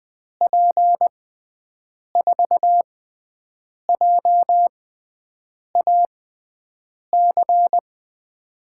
Morse code, heard as a keyed tone, about 20 wpm, spaced as for 7 wpm, 700 Hz. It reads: P4JAC
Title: USA Call Signs - 6 Characters - 6x Character Spacing - 20wpm